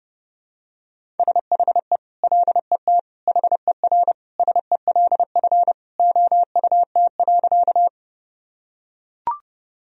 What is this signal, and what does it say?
Morse code, heard as a keyed tone, about 30 wpm, 700 Hz. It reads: SHE LET HERSELF OUT.
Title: Sentences from Top 1000 Words - Review Part 1 - 30wpm